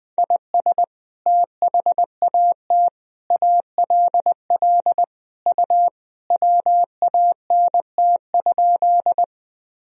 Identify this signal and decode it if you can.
Morse code, heard as a keyed tone, about 20 wpm, 700 Hz. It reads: IS THAT ALL U WANT?
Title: Sentences from Top 200 Words and Abbreviations 20wpm